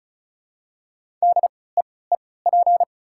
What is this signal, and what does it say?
Morse code, heard as a keyed tone, about 34 wpm, spaced as for 20 wpm, 690 Hz. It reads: DEEP
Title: 3x Character Spacing - Top 500 Words - 35wpm